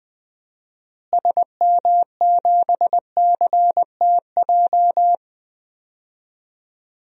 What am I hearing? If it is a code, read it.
Morse code, heard as a keyed tone, about 20 wpm, 700 Hz. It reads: SM7CTJ
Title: DXCC-100 Call Signs - 6 Characters - Repeated 2x - 20wpm